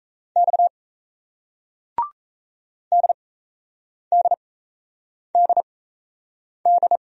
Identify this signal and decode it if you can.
Morse code, heard as a keyed tone, about 40 wpm, spaced as for 8 wpm, 700 Hz. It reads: X DDDD